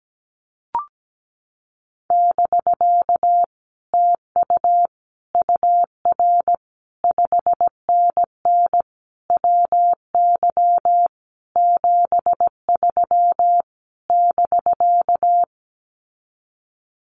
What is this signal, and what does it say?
Morse code, heard as a keyed tone, about 17 wpm, 700 Hz. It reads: <BK> TU UR 5NN WY 73 <BK>
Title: Common QSO Phrases Encore 17wpm